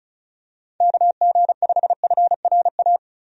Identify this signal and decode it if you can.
Morse code, heard as a keyed tone, about 36 wpm, 700 Hz. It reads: KG5FRA
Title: USA Call Signs - 6 Characters - 35wpm